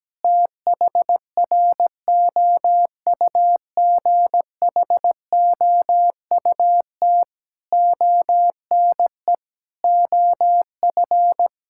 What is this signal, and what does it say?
Morse code, heard as a keyed tone, about 17 wpm, 700 Hz. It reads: THROUGHOUT ONE OF
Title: Sentences from Top 1000 Words - Review Part 5 - 17wpm